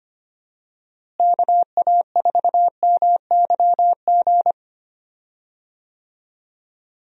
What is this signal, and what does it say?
Morse code, heard as a keyed tone, about 25 wpm, 700 Hz. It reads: KA4MYG
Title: USA Call Signs - 6 Characters - Repeated 3x - 25wpm